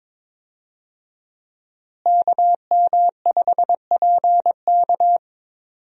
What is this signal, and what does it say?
Morse code, heard as a keyed tone, about 22 wpm, 700 Hz. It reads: KM5PK
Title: USA Call Signs - 5 Characters - Repeated 3x - 22wpm